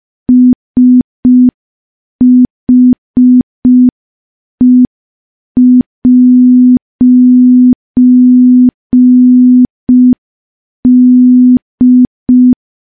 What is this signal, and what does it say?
Morse code, heard as a keyed tone, about 5 wpm, 250 Hz. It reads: SHE'D
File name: morse / morse-training-05WPM